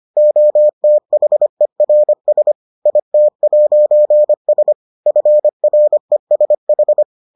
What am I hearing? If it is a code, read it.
Morse code, heard as a keyed tone, about 25 wpm, 600 Hz. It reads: OTHERS IT'S FRESH